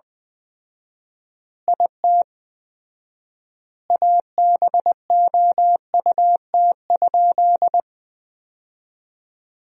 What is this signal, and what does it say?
Morse code, heard as a keyed tone, about 20 wpm, 700 Hz. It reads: IT ABOUT?
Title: 4x Word-Spacing - Sentences from Top 100 Words - 20wpm